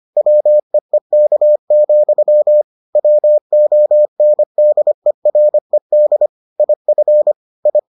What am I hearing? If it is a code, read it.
Morse code, heard as a keyed tone, about 25 wpm, 600 Hz. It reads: WEEK, WONDERED IF I